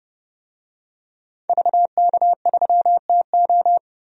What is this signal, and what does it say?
Morse code, heard as a keyed tone, about 30 wpm, 700 Hz. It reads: VK3TO
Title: DXCC-100 Call Signs - 5 Characters - Repeated 3x - 30wpm